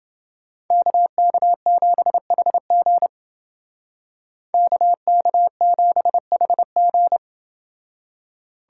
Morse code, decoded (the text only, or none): KK7HG KK7HG